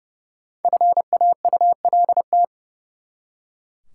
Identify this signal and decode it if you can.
Morse code, heard as a keyed tone, about 30 wpm, 700 Hz. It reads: FAULT
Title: Five Letter Words - 30wpm